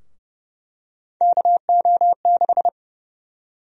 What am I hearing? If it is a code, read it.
Morse code, heard as a keyed tone, about 30 wpm, 700 Hz. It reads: KO6
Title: USA Call Sign Prefixes - Repeated 2x - 30wpm